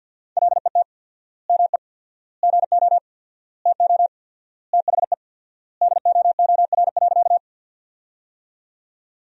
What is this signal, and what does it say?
Morse code, heard as a keyed tone, about 49 wpm, 700 Hz. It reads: LET ME GO TO THE DOOR.
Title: 4x Word-Spacing - Sentences from Top 300 Words - 50wpm